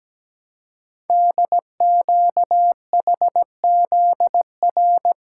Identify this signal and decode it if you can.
Morse code, heard as a keyed tone, about 17 wpm, 700 Hz. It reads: DQHZR
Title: Code Groups - 5 Letter:Numbers - 17wpm